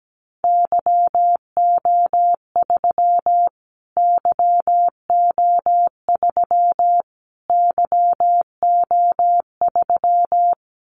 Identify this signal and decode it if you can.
Morse code, heard as a keyed tone, about 17 wpm, 700 Hz. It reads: YO3 YO3 YO3
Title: Callsign Prefixes for Top 25 DXCC - Repeated 3x - 17wpm